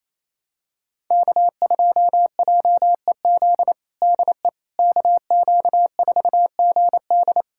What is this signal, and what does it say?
Morse code, heard as a keyed tone, about 27 wpm, 700 Hz. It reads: K2JEZ DE KQ4GD
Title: USA Callsign DE Callsign - 5 Characters - 28wpm